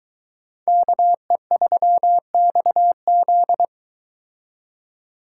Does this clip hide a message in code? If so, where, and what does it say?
Yes: Morse code from 0.7 to 3.6 s, heard as a keyed tone, about 22 wpm, 700 Hz. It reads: KE3XZ